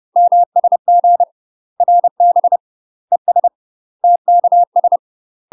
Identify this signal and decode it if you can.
Morse code, heard as a keyed tone, about 30 wpm, 700 Hz. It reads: MSG RB ES TKS
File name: UK MASTER QSOs TRAINING / 30WPM / UK MASTER QSOs 0016